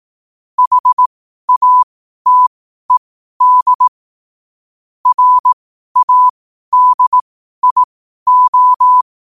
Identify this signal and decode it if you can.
Morse code, heard as a keyed tone, about 18 wpm, spaced as for 12 wpm, 1000 Hz. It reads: HATED RADIO
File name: qotdfiles / 12wpm